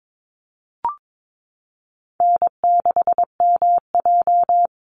Morse code, heard as a keyed tone, about 22 wpm, 700 Hz. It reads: N6MJ
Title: Easy Call Signs - 22wpm